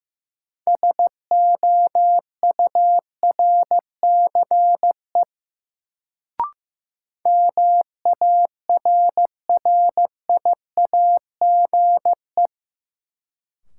Morse code, heard as a keyed tone, about 15 wpm, 700 Hz. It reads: SOURCE MARRIAGE
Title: Sentences from Top 800 Words - New Word Drill - 15wpm